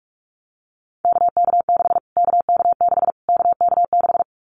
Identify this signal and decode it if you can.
Morse code, heard as a keyed tone, about 44 wpm, 700 Hz. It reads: KK6 KK6 KK6
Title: Callsign Prefixes for Top 25 DXCC - Repeated 3x - 45wpm